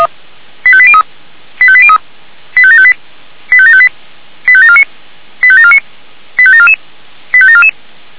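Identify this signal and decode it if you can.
Five-tone selective call (ZVEI), heard as quick runs of five tones, tones of about 70 ms. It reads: 85803 85803 85758 85758 85749 85749 85744 85744
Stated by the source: ZVEI five-tone selective call received on SDR.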